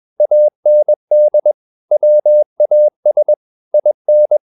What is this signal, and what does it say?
Morse code, heard as a keyed tone, about 21 wpm, 600 Hz. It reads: AND WAS IN
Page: CW med Gnister